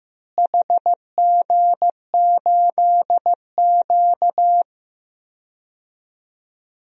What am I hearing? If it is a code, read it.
Morse code, heard as a keyed tone, about 15 wpm, 700 Hz. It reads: HG8Q
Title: DXCC-100 Call Signs - 4 Characters - Repeated 3x - 15wpm